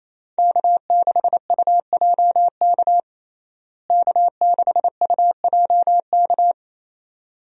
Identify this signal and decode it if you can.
Morse code, heard as a keyed tone, about 28 wpm, 700 Hz. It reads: K6UJK K6UJK
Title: USA Call Signs - 5 Characters - Repeated 2x - 28wpm